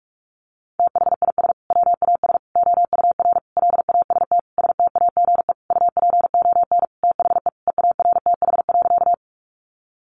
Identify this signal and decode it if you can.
Morse code, heard as a keyed tone, about 44 wpm, 700 Hz. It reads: THIS WAS OUR LAST STAGE UPON THE EARTH.
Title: Sentences from Top 1200 Words 45wpm